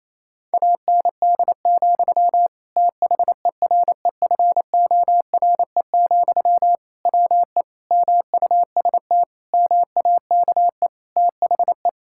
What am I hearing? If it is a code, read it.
Morse code, heard as a keyed tone, about 28 wpm, 700 Hz. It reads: AND, THEREFORE, WE MUST MAKE THE
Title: Sentences from Top 1000 Words - Review Part 5 - 28wpm